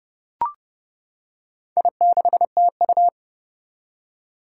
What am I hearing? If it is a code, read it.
Morse code, heard as a keyed tone, about 30 wpm, 700 Hz. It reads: I6TU
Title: Four Challenging Letter-Number Combinations 30wpm